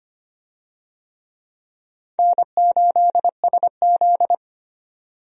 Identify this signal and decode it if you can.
Morse code, heard as a keyed tone, about 25 wpm, 700 Hz. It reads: N8SZ
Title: USA Call Signs - 4 Characters - 25wpm